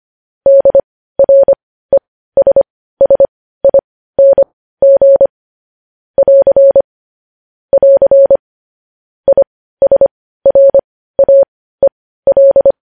Morse code, heard as a keyed tone, about 25 wpm, spaced as for 15 wpm, 550 Hz. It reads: DRESSING <AR> <AR> ISRAEL